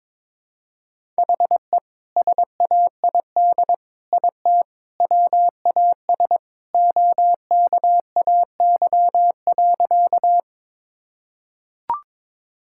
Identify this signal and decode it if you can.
Morse code, heard as a keyed tone, about 22 wpm, 700 Hz. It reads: HE SAID IT WAS OKAY.
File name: practice / 22wpm / Sentences with 2-Word Phrases - 22wpm